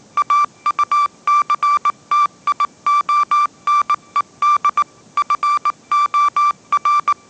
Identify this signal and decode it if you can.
Morse code, heard as a keyed tone, about 19 wpm, 1170 Hz. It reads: AUCTIONED FOR
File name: 20wpm